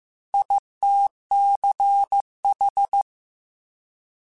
Morse code, heard as a keyed tone, about 15 wpm, 800 Hz. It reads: ITCH